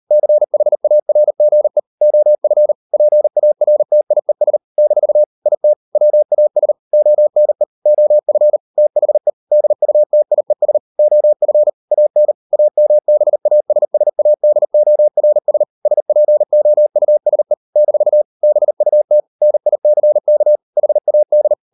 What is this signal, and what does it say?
Morse code, heard as a keyed tone, about 38 wpm, 600 Hz. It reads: CHARGE OF PARTIES - IT WAS ONE OF THE DUTIES OF AN AMBASSADORS SPOUSE - BUT NICK HAD